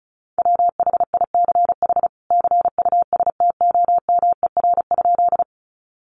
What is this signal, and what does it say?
Morse code, heard as a keyed tone, about 34 wpm, 700 Hz. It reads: WHICH CUSTOMER?
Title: Sentences from Top 1100 Words 35wpm